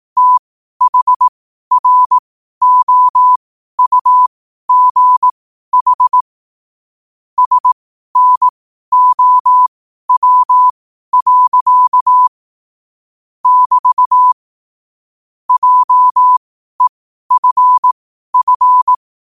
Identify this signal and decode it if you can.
Morse code, heard as a keyed tone, about 18 wpm, spaced as for 12 wpm, 1000 Hz. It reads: THROUGH SNOW. <BT> JEFF
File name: qotdfiles / 12wpm